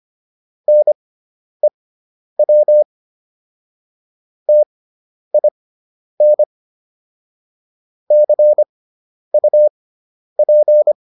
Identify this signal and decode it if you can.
Morse code, heard as a keyed tone, about 25 wpm, spaced as for 10 wpm, 600 Hz. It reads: NEW TIN CUP